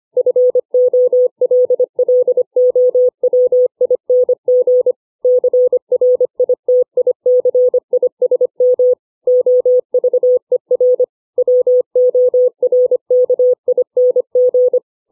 Morse code, heard as a keyed tone, about 25 wpm, 500 Hz. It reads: FOLLOWING CRITICISM OVER WORKING